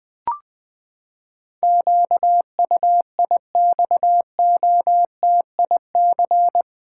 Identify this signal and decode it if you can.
Morse code, heard as a keyed tone, about 21 wpm, 700 Hz. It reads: QUIXOTIC
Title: Pangram Challenge v2 - 20wpm